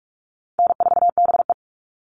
Morse code, heard as a keyed tone, about 44 wpm, 700 Hz. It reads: N4BE
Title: All Cut Numbers - 4-Digit 45wpm